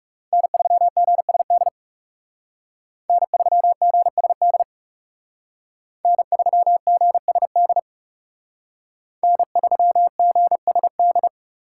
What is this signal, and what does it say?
Morse code, heard as a keyed tone, about 39 wpm, 700 Hz. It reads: N3GSD N3GSD N3GSD N3GSD